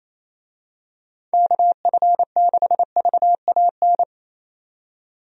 Morse code, heard as a keyed tone, about 28 wpm, 700 Hz. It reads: KF6VAN
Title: USA Call Signs - 6 Characters - 28wpm